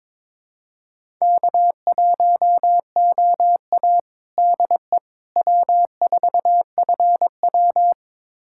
Morse code, heard as a keyed tone, about 22 wpm, 700 Hz. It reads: K1OA DE W4FW
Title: USA Callsign DE Callsign - 4 Characters - 22wpm